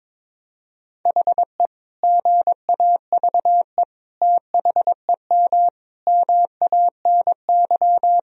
Morse code, heard as a keyed tone, about 22 wpm, 700 Hz. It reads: HE GAVE THEM MANY
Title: Sentences from Top 1000 Words - Review Part 3 - 22wpm